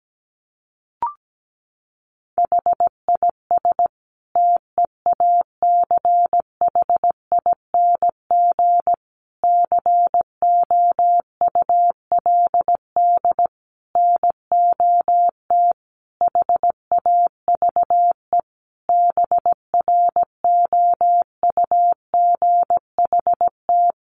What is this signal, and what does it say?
Morse code, heard as a keyed tone, about 17 wpm, 700 Hz. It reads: HIS TEACHING COULD NOT HAVE BROUGHT
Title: Sentences from Top 1100 Words 17wpm